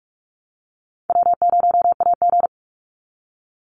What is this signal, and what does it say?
Morse code, heard as a keyed tone, about 46 wpm, 700 Hz. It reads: W0AG
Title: Easy Call Signs - 45wpm